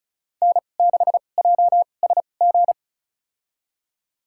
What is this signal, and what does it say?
Morse code, heard as a keyed tone, about 36 wpm, spaced as for 25 wpm, 700 Hz. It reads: N6JSG
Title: USA Call Signs - 2x Character Spacing - 35wpm